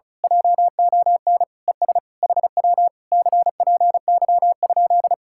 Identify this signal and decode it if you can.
Morse code, heard as a keyed tone, about 33 wpm, 700 Hz. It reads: JON ES HW CPY?
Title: Speed-Racing Common QSO Phrases Encore - 35wpm